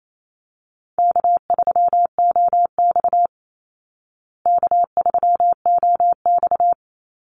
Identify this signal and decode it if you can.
Morse code, heard as a keyed tone, about 28 wpm, 700 Hz. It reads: K3OX K3OX
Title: Easy Call Signs Repeated 2x - 28wpm